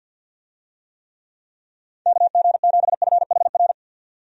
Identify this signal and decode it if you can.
Morse code, heard as a keyed tone, about 50 wpm, 700 Hz. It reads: KG7FHR
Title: USA Call Signs - 6 Characters - 50wpm